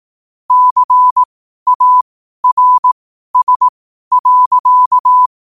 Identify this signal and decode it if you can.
Morse code, heard as a keyed tone, about 18 wpm, spaced as for 12 wpm, 1000 Hz. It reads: CARS.
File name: qotdfiles / 12wpm